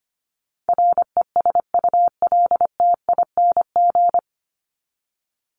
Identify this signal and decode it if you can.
Morse code, heard as a keyed tone, about 25 wpm, 700 Hz. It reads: RESULTING
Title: ING Suffix - Top 100 Words with Base Words 25wpm